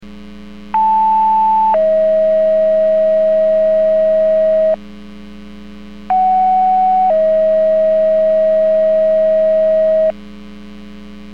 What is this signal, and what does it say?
Two-tone page, heard as a one-second tone, then a three-second tone, about 880 then 640 Hz, about 750 then 640 Hz. It is heard from 0.7 to 10.1 s.
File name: 21-25 Portage 154.385 Paging